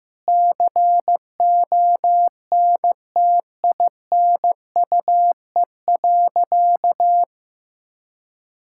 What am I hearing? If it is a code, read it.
Morse code, heard as a keyed tone, about 16 wpm, 700 Hz. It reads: CONTINUE.
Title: Sentences from Top 1000 Words - Review Part 6 - 15wpm